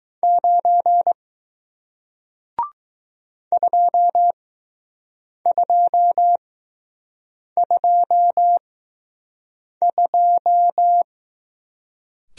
Morse code, heard as a keyed tone, about 20 wpm, spaced as for 6 wpm, 700 Hz. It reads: 9 2222